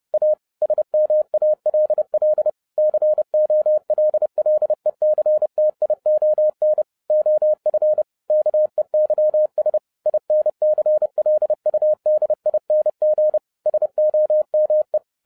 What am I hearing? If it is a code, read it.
Morse code, heard as a keyed tone, about 30 wpm, 610 Hz. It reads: A SMALL COLLECTION OF KEYS INCLUDING SOME
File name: 2011-07-mp3 / n20 GREETINGS FROM SINGAPORE 30wpm